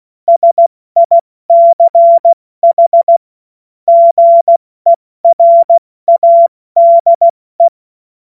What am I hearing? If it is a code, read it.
Morse code, heard as a keyed tone, about 16 wpm, spaced as for 14 wpm, 680 Hz. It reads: SICH GERADE
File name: Morsekurs / Witze / Witz 11. Januar 2022, 70 F80